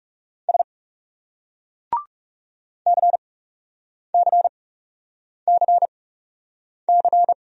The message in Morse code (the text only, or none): S CCCC